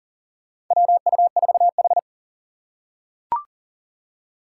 Morse code, heard as a keyed tone, about 40 wpm, 700 Hz. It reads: WU4H
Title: USA Call Signs - 4 Characters - 2x Character Spacing - 40wpm